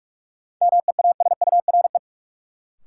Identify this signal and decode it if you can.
Morse code, heard as a keyed tone, about 46 wpm, 700 Hz. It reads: MEASURE
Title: Sentences from Top 1200 Words - New Word Drill Alt - 45wpm